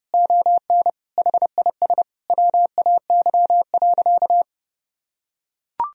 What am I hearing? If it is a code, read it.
Morse code, heard as a keyed tone, about 30 wpm, 700 Hz. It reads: ON HIS WAY.
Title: Sentences from Top 300 Words and Abbreviations 30wpm